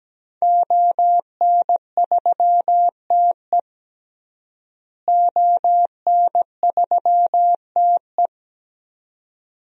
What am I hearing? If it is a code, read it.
Morse code, heard as a keyed tone, about 17 wpm, 700 Hz. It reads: ON3TE ON3TE
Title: DXCC-100 Call Signs - 5 Characters - Repeated 2x - 17wpm